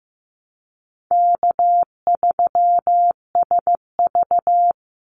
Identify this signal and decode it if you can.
Morse code, heard as a keyed tone, about 15 wpm, 700 Hz. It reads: K3SV
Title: Easy Call Signs - 15wpm